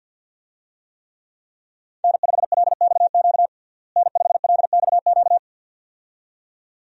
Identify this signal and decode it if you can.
Morse code, heard as a keyed tone, about 53 wpm, 700 Hz. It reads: N5LXQ N5LXQ
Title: USA Call Signs - 5 Characters - Repeated 2x - 50wpm